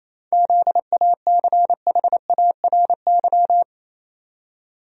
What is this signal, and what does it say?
Morse code, heard as a keyed tone, about 28 wpm, 700 Hz. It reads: ZACHARY
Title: USA Common Names - 28wpm